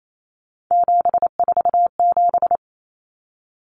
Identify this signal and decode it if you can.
Morse code, heard as a keyed tone, about 28 wpm, 700 Hz. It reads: 747
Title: Numbers - 3-Digit 28wpm